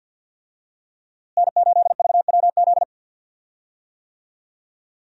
Morse code, heard as a keyed tone, about 52 wpm, 700 Hz. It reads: N8VWZ
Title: USA Call Signs - 5 Characters - Repeated 3x - 50wpm